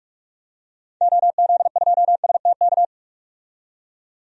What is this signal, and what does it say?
Morse code, heard as a keyed tone, about 47 wpm, 700 Hz. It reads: OZ2STK
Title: DXCC-100 Call Signs - 6 Characters - 45wpm